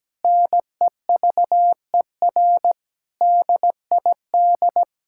Morse code, heard as a keyed tone, about 17 wpm, 700 Hz. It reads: NEVER DID
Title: Sentences from Top 1000 Words - Review Part 5 - 17wpm